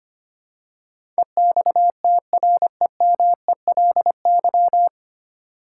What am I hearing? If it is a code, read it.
Morse code, heard as a keyed tone, about 25 wpm, 700 Hz. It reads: EXTREMELY
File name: Nine Letter Words - 25wpm